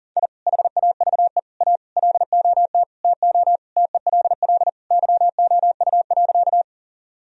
Morse code, heard as a keyed tone, about 37 wpm, 700 Hz. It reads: I HAVE A LOT TO TELL YOU.
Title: Sentences from Top 1000 Words - Review Part 3 - 40wpm